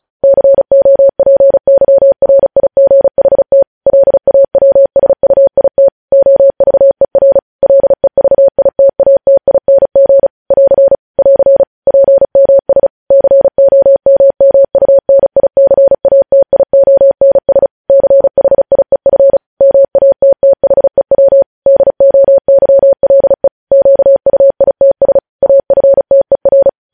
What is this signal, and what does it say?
Morse code, heard as a keyed tone, about 36 wpm, 550 Hz. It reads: COPYRIGHT LAWSUIT OVER LEVITATING <AR> <AR> PMS COMMUNICATIONS CHIEF MATTHEW DOYLE QUITS AFTER
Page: News headlines in Morse code at 35 wpm